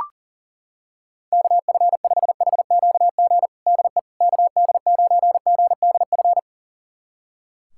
Morse code, heard as a keyed tone, about 40 wpm, 700 Hz. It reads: KF5HQG DE KD9GDF
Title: USA Callsign DE Callsign 40wpm